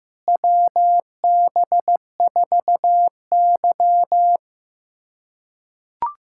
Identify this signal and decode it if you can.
Morse code, heard as a keyed tone, about 15 wpm, 700 Hz. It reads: WB4Y
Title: USA Callsign DE Callsign - 4 Characters - 15wpm